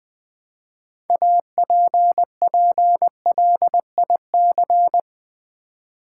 Morse code, heard as a keyed tone, about 20 wpm, 700 Hz. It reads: APPLIC
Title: Build A Word - 3 Letters At A Time - 9-Letter Words - 20wpm